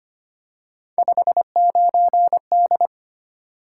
Morse code, heard as a keyed tone, about 25 wpm, 700 Hz. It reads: 59D
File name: Three Letter-Number Combinations - 25wpm